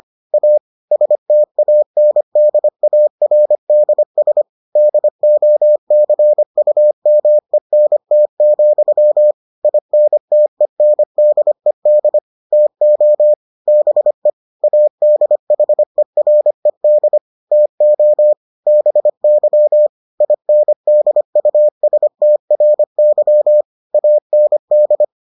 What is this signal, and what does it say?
Morse code, heard as a keyed tone, about 25 wpm, 600 Hz. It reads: A STANDARDS DOCUMENT, INTENDED TO BE ADHERED TO BY INDUSTRY AND